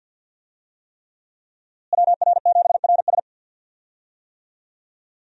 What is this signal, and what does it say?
Morse code, heard as a keyed tone, about 47 wpm, 700 Hz. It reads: WR7RS
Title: USA Call Signs - 5 Characters - Repeated 3x - 50wpm